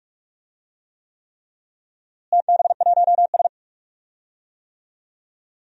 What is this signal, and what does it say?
Morse code, heard as a keyed tone, about 46 wpm, 700 Hz. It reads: TB1S
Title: DXCC-100 Call Signs - 4 Characters - Repeated 3x - 45wpm